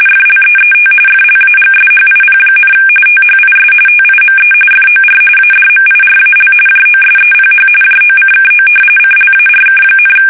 RTTY, about 45.45 baud. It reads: FLEET FLAGSHIP, THIS TIME FOR ADMIRAL JOSEPH JOCKO CLARK,
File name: fsk850 / battleshipnj